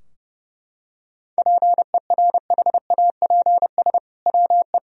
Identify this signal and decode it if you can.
Morse code, heard as a keyed tone, about 30 wpm, 700 Hz. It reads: PERHAPS WE
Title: Sentences from Top 1000 Words - Review Part 3 - 30wpm